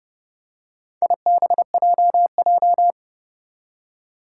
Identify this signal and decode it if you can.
Morse code, heard as a keyed tone, about 30 wpm, 690 Hz. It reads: IBJJ
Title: Four Challenging Letter-Number Combinations 30wpm